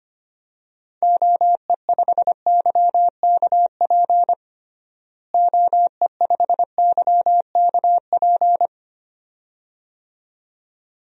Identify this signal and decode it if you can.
Morse code, heard as a keyed tone, about 25 wpm, 700 Hz. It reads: OE5YKP OE5YKP
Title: DXCC-100 Call Signs - 6 Characters - Repeated 2x - 25wpm